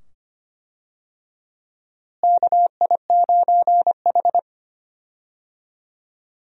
Morse code, heard as a keyed tone, about 25 wpm, 700 Hz. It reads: KI9H
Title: USA Call Signs - 4 Characters - Repeated 2x - 25wpm